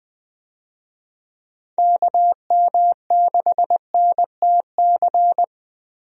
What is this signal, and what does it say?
Morse code, heard as a keyed tone, about 20 wpm, 700 Hz. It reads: KM6NTC
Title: USA Call Signs - 6 Characters - Repeated 3x - 20wpm